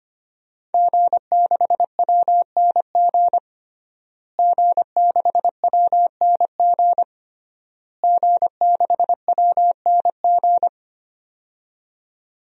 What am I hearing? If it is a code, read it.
Morse code, heard as a keyed tone, about 25 wpm, 700 Hz. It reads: G6WNG G6WNG G6WNG
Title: DXCC-100 Call Signs - 5 Characters - Repeated 3x - 25wpm